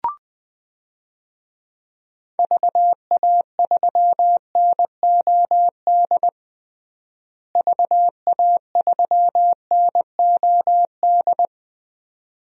Morse code, heard as a keyed tone, about 20 wpm, 700 Hz. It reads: VA3NOD VA3NOD
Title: DXCC-100 Call Signs - 6 Characters - Repeated 2x - 20wpm